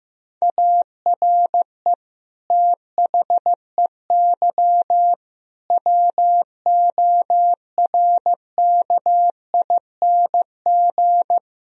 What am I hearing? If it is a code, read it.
Morse code, heard as a keyed tone, about 15 wpm, 700 Hz. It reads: ARE THEY WORKING
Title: Sentences from Top 1000 Words - Review Part 4 - 15wpm